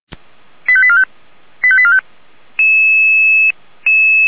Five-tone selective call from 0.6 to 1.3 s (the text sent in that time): 85845